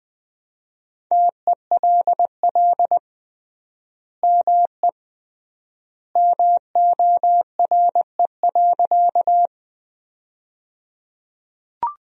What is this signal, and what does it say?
Morse code, heard as a keyed tone, about 20 wpm, 700 Hz. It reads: TELL ME MORE.
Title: 3x Word-Spacing - Sentences from Top 300 Words - 20wpm